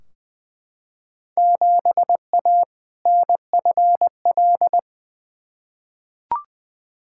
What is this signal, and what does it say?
Morse code, heard as a keyed tone, about 20 wpm, 700 Hz. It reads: 7A NFL
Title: Field Day Exchanges Encore - 20wpm